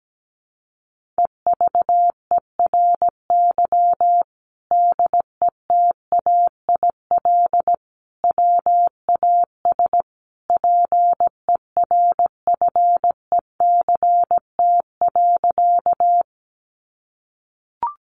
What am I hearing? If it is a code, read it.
Morse code, heard as a keyed tone, about 17 wpm, 700 Hz. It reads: EVERY DETAIL WAS PERFECT.
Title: Sentences from Top 1500 Words 17wpm